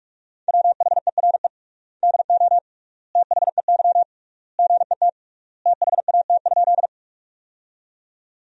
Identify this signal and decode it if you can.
Morse code, heard as a keyed tone, about 46 wpm, 700 Hz. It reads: WHERE DO THEY GET THAT?
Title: 3x Word-Spacing - Sentences from Top 200 Words - 45wpm